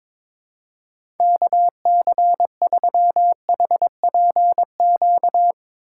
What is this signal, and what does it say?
Morse code, heard as a keyed tone, about 21 wpm, 700 Hz. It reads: KC3HPQ
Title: USA Call Signs - 6 Characters - 3x Character Spacing - 22wpm